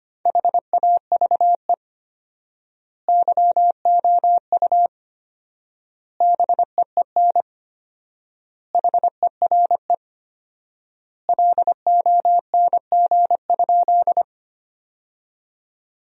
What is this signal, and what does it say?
Morse code, heard as a keyed tone, about 25 wpm, 700 Hz. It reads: HAVE YOU BEEN HERE LONG?
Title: 4x Word-Spacing - Sentences from Top 200 Words - 25wpm